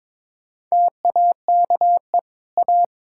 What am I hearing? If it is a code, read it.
Morse code, heard as a keyed tone, about 22 wpm, 700 Hz. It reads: TAKE A
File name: practice / 22wpm / Sets of 6 Words - 22wpm